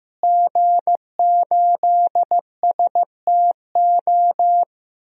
Morse code, heard as a keyed tone, about 15 wpm, 700 Hz. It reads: G8STO
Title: DXCC-100 Call Signs - 5 Characters - Repeated 3x - 15wpm